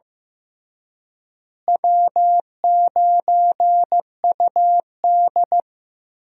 Morse code, heard as a keyed tone, about 16 wpm, 700 Hz. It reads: W9UD
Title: USA Call Signs - 4 Characters - Repeated 3x - 15wpm